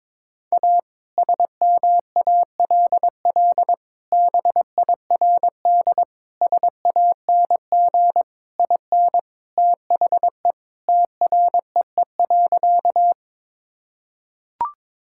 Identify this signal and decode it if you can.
Morse code, heard as a keyed tone, about 22 wpm, 700 Hz. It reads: A SMALL BIRD SANG IN THE TREE.
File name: practice / 22wpm / Sentences with 2-Word Phrases - 22wpm